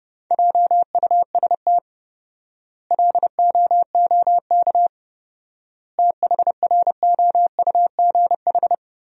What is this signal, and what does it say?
Morse code, heard as a keyed tone, about 30 wpm, 700 Hz. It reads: JUST LOOK THROUGH
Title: 4x Word-Spacing - Sentences from Top 200 Words - 30wpm